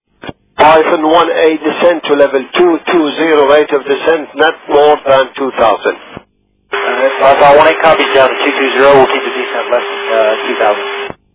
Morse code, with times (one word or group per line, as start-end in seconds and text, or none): none